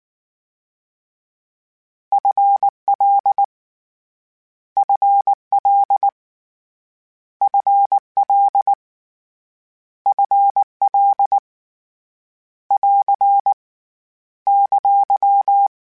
Morse code, heard as a keyed tone, about 19 wpm, 800 Hz. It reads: FL FL FL FL <AR> !
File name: Short Runs / Short Runs-mp3s / ShortContrasts @18